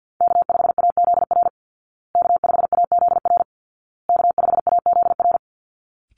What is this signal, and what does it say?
Morse code, heard as a keyed tone, about 50 wpm, 700 Hz. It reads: K5AZR K5AZR K5AZR
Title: Call Signs Repeated 2x - 50wpm